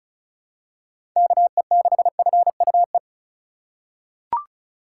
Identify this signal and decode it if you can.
Morse code, heard as a keyed tone, about 36 wpm, 700 Hz. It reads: KE6FUE
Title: USA Call Signs - 6 Characters - 4x Character Spacing - 35wpm